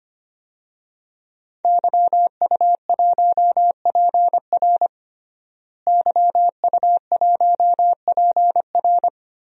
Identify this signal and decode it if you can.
Morse code, heard as a keyed tone, about 25 wpm, 700 Hz. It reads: YU1PR YU1PR
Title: DXCC-100 Call Signs - 5 Characters - Repeated 2x - 25wpm